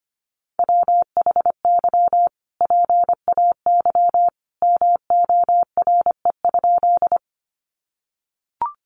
Morse code, heard as a keyed tone, about 25 wpm, 700 Hz. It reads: WHY PAY MORE?
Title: Sentences from Top 700 Words 25wpm